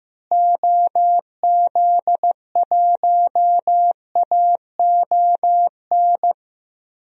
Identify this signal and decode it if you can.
Morse code, heard as a keyed tone, about 15 wpm, 700 Hz. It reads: OZ1AON
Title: DXCC-100 Call Signs - 6 Characters - Repeated 3x - 15wpm